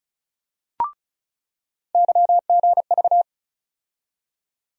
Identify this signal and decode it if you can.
Morse code, heard as a keyed tone, about 34 wpm, 700 Hz. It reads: YGV
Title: USA Call Sign Postfixes - 3 Letters 35wpm